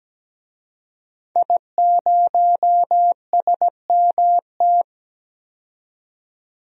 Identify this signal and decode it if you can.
Morse code, heard as a keyed tone, about 17 wpm, 700 Hz. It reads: I0SMT
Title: DXCC-100 Call Signs - 5 Characters - 17wpm